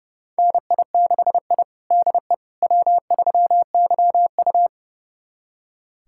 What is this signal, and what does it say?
Morse code, heard as a keyed tone, about 32 wpm, 700 Hz. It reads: NI6I DE W3YU
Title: USA Callsign DE Callsign - 4 Characters - 30wpm